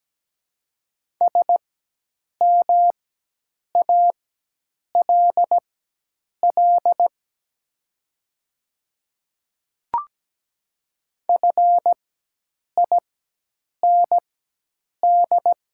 Morse code, heard as a keyed tone, about 17 wpm, spaced as for 8 wpm, 700 Hz. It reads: SMALL FIND
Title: 4x Character Spacing - Top 300 Words - 17wpm